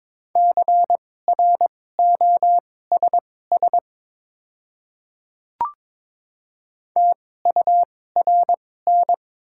Morse code, heard as a keyed tone, about 22 wpm, spaced as for 16 wpm, 700 Hz. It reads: CROSS TURN
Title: 2x Character Spacing - Top 300 Words - 22wpm